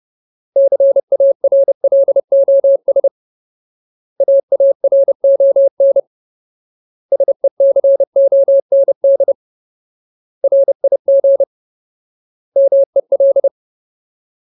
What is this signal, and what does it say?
Morse code, heard as a keyed tone, about 30 wpm, 550 Hz. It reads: CARLOS AARON SECOND RIG MEL
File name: Ultimate Word List / 30WPM Ultimate